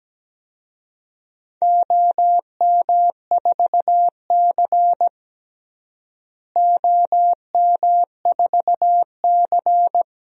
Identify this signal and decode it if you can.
Morse code, heard as a keyed tone, about 17 wpm, 700 Hz. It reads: OM4C OM4C
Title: DXCC-100 Call Signs - 4 Characters - Repeated 3x - 17wpm